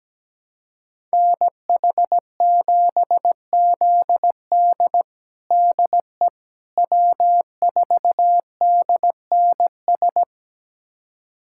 Morse code, heard as a keyed tone, about 17 wpm, 700 Hz. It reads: NH7ZD DE W4DNS
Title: USA Callsign DE Callsign - 5 Characters - 17wpm